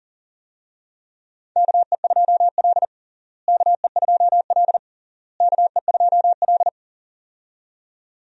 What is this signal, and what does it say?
Morse code, heard as a keyed tone, about 40 wpm, 700 Hz. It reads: KE2L KE2L KE2L
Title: USA Call Signs - 4 Characters - Repeated 3x - 40wpm